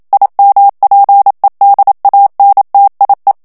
Morse code, heard as a keyed tone, about 27 wpm, 800 Hz. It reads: IMPEDANTIE
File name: HOLLANDAIS texte clair 1 vitesse 26 mots minute